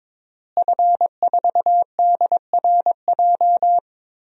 Morse code, heard as a keyed tone, about 22 wpm, 700 Hz. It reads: F4DRJ
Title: DXCC-100 Call Signs - 5 Characters - Repeated 3x - 22wpm